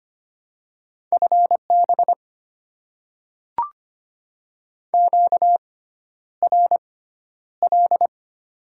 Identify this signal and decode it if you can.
Morse code, heard as a keyed tone, about 25 wpm, spaced as for 11 wpm, 700 Hz. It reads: FB QRL
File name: Top 90 QSO Elements - 6x Character Spacing - 25wpm